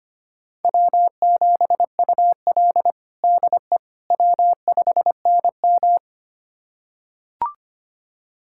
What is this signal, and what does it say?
Morse code, heard as a keyed tone, about 25 wpm, 700 Hz. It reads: W7UL DE W5NM
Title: USA Callsign DE Callsign - 4 Characters - 25wpm